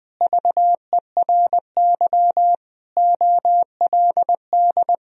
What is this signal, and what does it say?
Morse code, heard as a keyed tone, about 20 wpm, 700 Hz. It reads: VERY OLD
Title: Sentences from Top 1000 Words - Review Part 3 - 20wpm